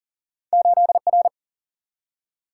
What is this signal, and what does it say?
Morse code, heard as a keyed tone, about 40 wpm, 700 Hz. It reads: 8R
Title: Single-Letter Single-Number 40wpm